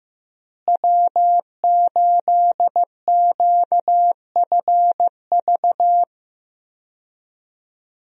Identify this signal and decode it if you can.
Morse code, heard as a keyed tone, about 15 wpm, 700 Hz. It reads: W8QFV
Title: USA Call Signs - 5 Characters - Repeated 3x - 15wpm